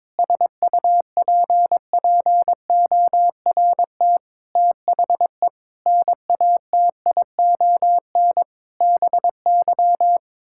Morse code, heard as a keyed tone, about 22 wpm, 700 Hz. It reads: SUPPORT THE NATION BY
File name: Sets of 9 Words - 22wpm